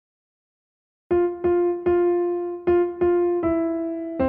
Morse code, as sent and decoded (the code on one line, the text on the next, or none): none